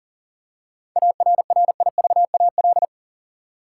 Morse code, heard as a keyed tone, about 40 wpm, 700 Hz. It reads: ARRIVAL
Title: Seven Letter Words - 40wpm